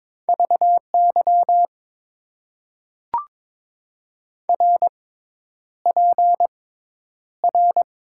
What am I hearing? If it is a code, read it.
Morse code, heard as a keyed tone, about 22 wpm, spaced as for 10 wpm, 700 Hz. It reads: VY RPR
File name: Top 90 QSO Elements - 6x Character Spacing - 22wpm